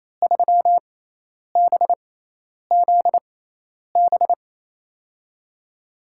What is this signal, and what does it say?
Morse code, heard as a keyed tone, about 28 wpm, spaced as for 10 wpm, 700 Hz. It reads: 3BZB
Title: USA Call Signs - 6 Characters - 6x Character Spacing - 28wpm